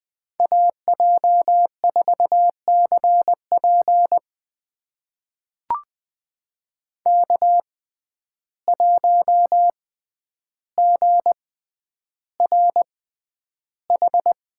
Morse code, heard as a keyed tone, about 20 wpm, spaced as for 10 wpm, 700 Hz. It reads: AJ4CP K1GRH
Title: USA Call Signs - 5 Characters - 6x Character Spacing - 20wpm